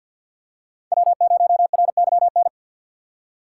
Morse code, heard as a keyed tone, about 50 wpm, 700 Hz. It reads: W0RYN
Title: USA Call Signs - 5 Characters - 3x Character Spacing - 50wpm